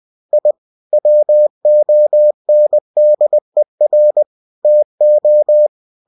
Morse code, heard as a keyed tone, about 20 wpm, 600 Hz. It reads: I WONDER TO
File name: cw-books / winnie-the-pooh / winnie-20wpm-page-0068